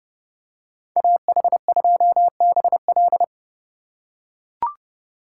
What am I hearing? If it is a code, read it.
Morse code, heard as a keyed tone, about 29 wpm, 700 Hz. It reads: AH2BL